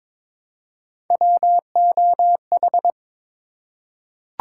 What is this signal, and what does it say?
Morse code, heard as a keyed tone, about 23 wpm, 700 Hz. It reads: WOH
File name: practice / 22wpm / Three Letter-Number Combinations - 22wpm